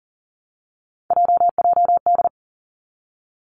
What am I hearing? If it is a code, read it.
Morse code, heard as a keyed tone, about 40 wpm, 700 Hz. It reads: JJD
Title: Three Challenging Letter-Number Combinations 40wpm